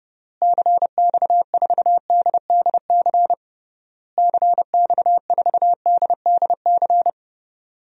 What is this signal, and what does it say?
Morse code, heard as a keyed tone, about 30 wpm, 700 Hz. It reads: CX4DDC CX4DDC
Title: DXCC-100 Call Signs - 6 Characters - Repeated 3x - 30wpm